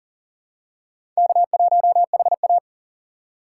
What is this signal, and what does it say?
Morse code, heard as a keyed tone, about 40 wpm, 700 Hz. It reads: K1HA
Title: USA Call Signs - 4 Characters - 2x Character Spacing - 40wpm